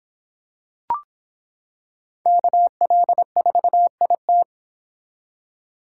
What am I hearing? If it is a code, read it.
Morse code, heard as a keyed tone, about 25 wpm, 700 Hz. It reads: KL4IT